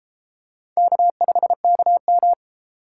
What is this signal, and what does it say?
Morse code, heard as a keyed tone, about 33 wpm, 700 Hz. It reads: K5KM